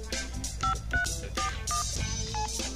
DTMF, heard as telephone touch tones, tones of about 120 ms. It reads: 5341